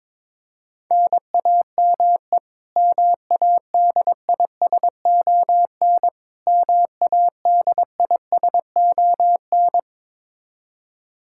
Morse code, heard as a keyed tone, about 22 wpm, 700 Hz. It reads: NAME MADISON MADISON
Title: K1USN SST QSO Practice 22wpm